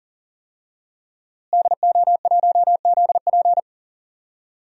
Morse code, heard as a keyed tone, about 40 wpm, 700 Hz. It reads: DO1ZP
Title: DXCC-100 Call Signs - 5 Characters - Repeated 3x - 40wpm